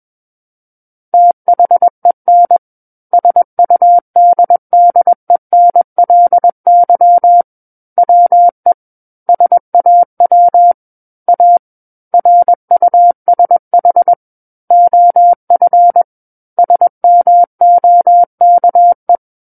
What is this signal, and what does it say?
Morse code, heard as a keyed tone, about 21 wpm, 700 Hz. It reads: THEN SUDDENLY WE SAW A RUSH OF SMOKE
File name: Book One / 12-Chapter 12 - 21 WPM